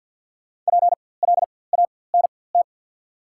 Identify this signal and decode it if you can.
Morse code, heard as a keyed tone, about 53 wpm, spaced as for 24 wpm, 690 Hz. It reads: PLANT
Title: 4x Character Spacing - Top 300 Words - 50wpm